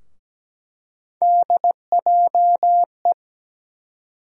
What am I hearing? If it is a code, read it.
Morse code, heard as a keyed tone, about 17 wpm, 700 Hz. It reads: DJE
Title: USA Call Sign Postfixes - 3 Letters 17wpm